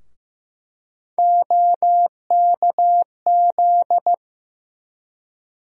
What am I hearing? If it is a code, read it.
Morse code, heard as a keyed tone, about 15 wpm, 700 Hz. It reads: OKZ
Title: USA Call Sign Postfixes - 3 Letters 15wpm